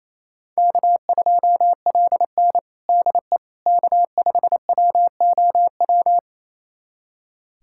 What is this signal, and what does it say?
Morse code, heard as a keyed tone, about 28 wpm, 700 Hz. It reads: K2LN DE K5WOW
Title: USA Callsign DE Callsign 28wpm